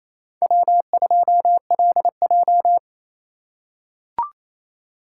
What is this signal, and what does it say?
Morse code, heard as a keyed tone, about 28 wpm, 700 Hz. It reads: W2LJ
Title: USA Call Signs - 4 Characters - 3x Character Spacing - 28wpm